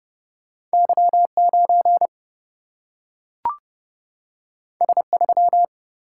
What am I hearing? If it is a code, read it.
Morse code, heard as a keyed tone, about 30 wpm, 700 Hz. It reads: Y9 S3
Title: Single-Letter Single-Number 30wpm